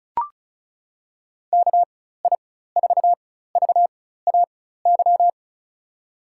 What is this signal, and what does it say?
Morse code, heard as a keyed tone, about 36 wpm, spaced as for 16 wpm, 700 Hz. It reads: KI4VAY
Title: USA Call Signs - 6 Characters - 4x Character Spacing - 35wpm